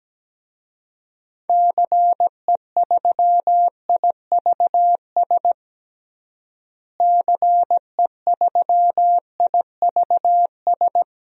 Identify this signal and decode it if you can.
Morse code, heard as a keyed tone, about 17 wpm, 700 Hz. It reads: CE3IVS CE3IVS
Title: DXCC-100 Call Signs - 6 Characters - Repeated 2x - 17wpm